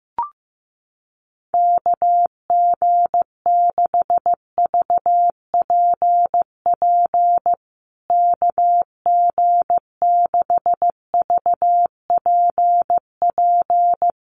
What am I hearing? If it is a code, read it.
Morse code, heard as a keyed tone, about 15 wpm, 700 Hz. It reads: KG6VPP KG6VPP
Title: USA Call Signs Repeated 3x 15wpm